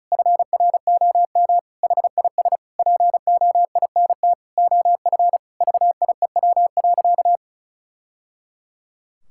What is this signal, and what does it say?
Morse code, heard as a keyed tone, about 34 wpm, 700 Hz. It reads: FROM HIS POINT OF VIEW.
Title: Sentences from Top 1000 Words - Review Part 6 - 35wpm